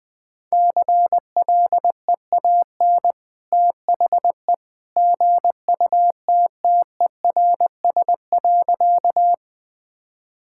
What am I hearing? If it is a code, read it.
Morse code, heard as a keyed tone, about 21 wpm, 700 Hz. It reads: CLEAN THE GUTTERS.
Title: Easy Sentences with Hints - 20wpm